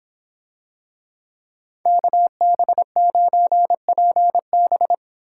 Morse code, heard as a keyed tone, about 26 wpm, 700 Hz. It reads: KB9PB